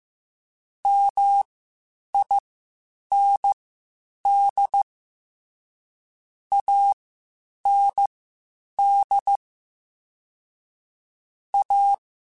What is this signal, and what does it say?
Morse code, heard as a keyed tone, about 15 wpm, spaced as for 8 wpm, 800 Hz. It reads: MIND AND A
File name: Spaced / Chuckles Spaced-mp3s / MiddleAge sp @13